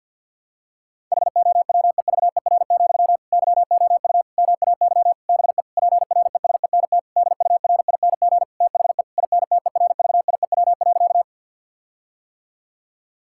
Morse code, heard as a keyed tone, about 52 wpm, 700 Hz. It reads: HOWEVER, YOU MAY BE PRESENT DURING THE INTERVIEW.
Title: Sentences from Top 1600 Words 50wpm